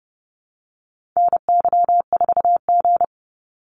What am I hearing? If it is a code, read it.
Morse code, heard as a keyed tone, about 30 wpm, 700 Hz. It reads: NY4G
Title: NA SOTA Activator Callsigns 30wpm